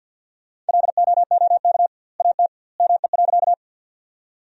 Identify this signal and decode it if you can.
Morse code, heard as a keyed tone, about 53 wpm, 700 Hz. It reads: LOOK AT ME.
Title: 2x Word-Spacing - Sentences from Top 200 Words - 50wpm